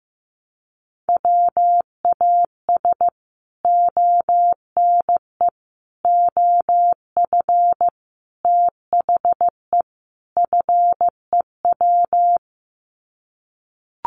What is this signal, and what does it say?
Morse code, heard as a keyed tone, about 15 wpm, 700 Hz. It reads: WAS ONE OF THE FEW
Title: Sets of 5 Words Encore 15wpm